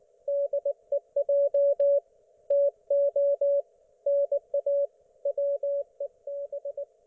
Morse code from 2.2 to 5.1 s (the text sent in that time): TO NA